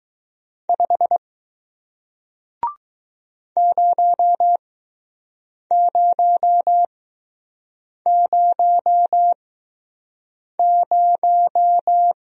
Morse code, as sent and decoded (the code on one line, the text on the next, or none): ..... / ----- ----- ----- -----
5 0000